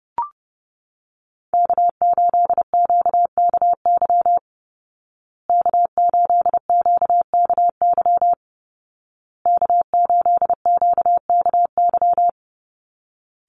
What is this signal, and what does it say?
Morse code, heard as a keyed tone, about 30 wpm, 700 Hz. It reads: K8QKY K8QKY K8QKY
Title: Call Signs Repeated 2x - 30wpm